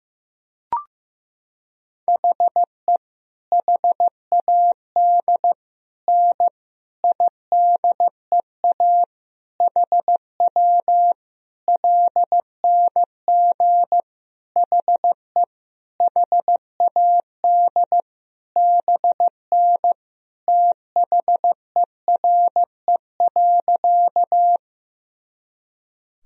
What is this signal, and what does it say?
Morse code, heard as a keyed tone, about 16 wpm, 700 Hz. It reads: HE HAD N IDEA HW LNG HE HAD BN THERE.
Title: Sentences from Top 300 Words and Abbreviations 15wpm